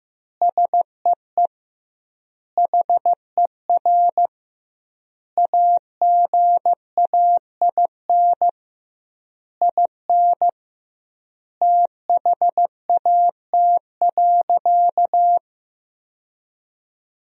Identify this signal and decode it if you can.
Morse code, heard as a keyed tone, about 16 wpm, 700 Hz. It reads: SEE HER AGAIN IN THAT.
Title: 2x Word-Spacing - Sentences from Top 200 Words - 15wpm